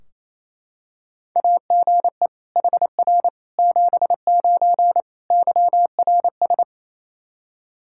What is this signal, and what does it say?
Morse code, heard as a keyed tone, about 28 wpm, 700 Hz. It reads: AGE HR 79 YRS
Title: Practice QSO - Exchange 3 - More Info - 28wpm